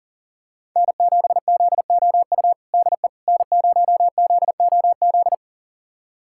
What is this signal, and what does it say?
Morse code, heard as a keyed tone, about 40 wpm, 700 Hz. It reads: N7ZOU DE N0ZOZ
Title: USA Callsign DE Callsign - 5 Characters - 40wpm